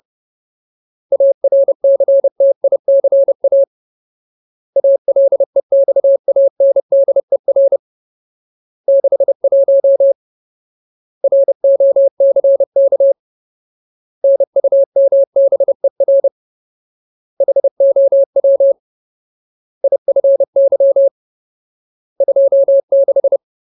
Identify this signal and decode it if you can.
Morse code, heard as a keyed tone, about 30 wpm, 550 Hz. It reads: ARCTICA ALEXANDER 61 ROCK NUMBER HOW IFY 26
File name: Ultimate Word List / 30WPM Ultimate